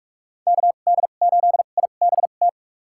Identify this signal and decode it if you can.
Morse code, heard as a keyed tone, about 46 wpm, spaced as for 33 wpm, 700 Hz. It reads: KD8IBT
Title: USA Call Signs - 6 Characters - 2x Character Spacing - 45wpm